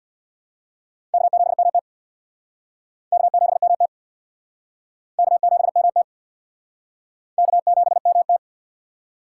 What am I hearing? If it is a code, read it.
Morse code, heard as a keyed tone, about 64 wpm, 700 Hz. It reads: K7MT K7MT K7MT K7MT